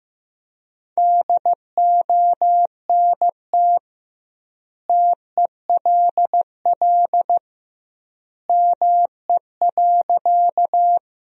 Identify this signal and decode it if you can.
Morse code, heard as a keyed tone, about 15 wpm, 700 Hz. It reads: DONT TELL ME.
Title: 2x Word-Spacing - Sentences from Top 300 Words - 15wpm